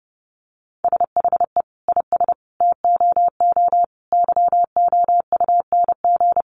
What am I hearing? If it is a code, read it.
Morse code, heard as a keyed tone, about 29 wpm, 700 Hz. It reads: SHE IS TOO YOUNG
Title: Sentences from Top 1300 Words 30wpm